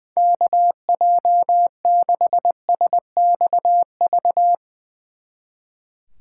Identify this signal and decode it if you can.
Morse code, heard as a keyed tone, about 22 wpm, 700 Hz. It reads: KJ6SXV
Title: Speed Racing - USA Call Signs 20wpm